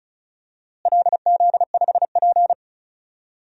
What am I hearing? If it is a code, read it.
Morse code, heard as a keyed tone, about 36 wpm, 700 Hz. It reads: LZ5P
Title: DXCC-100 Call Signs - 4 Characters - Repeated 2x - 35wpm